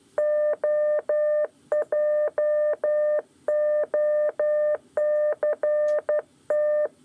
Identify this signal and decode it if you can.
Morse code, heard as a keyed tone, about 12 wpm, 600 Hz. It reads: OJOCT
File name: 8 lezione cw 1